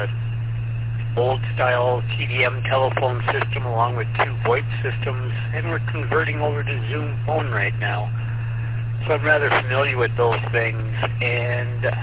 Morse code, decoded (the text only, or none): none